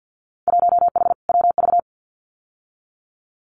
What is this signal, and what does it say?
Morse code, heard as a keyed tone, about 50 wpm, 700 Hz. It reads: 1H WV
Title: Winter Field Day Exchanges 50wpm